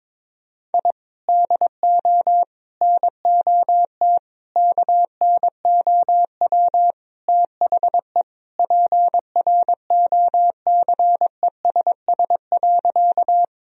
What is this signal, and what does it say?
Morse code, heard as a keyed tone, about 22 wpm, 700 Hz. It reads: I DO NOT KNOW THE PROCESS.
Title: Sentences from Top 1000 Words - Review Part 3 - 22wpm